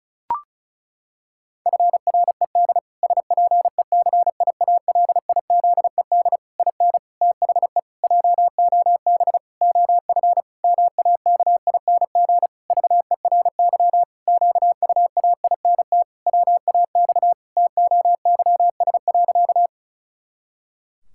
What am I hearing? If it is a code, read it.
Morse code, heard as a keyed tone, about 36 wpm, 700 Hz. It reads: FRED SPECIALIZED IN THE JOB OF MAKING VERY QUAINT WAX TOYS.
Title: Pangram Challenge v2 - 35wpm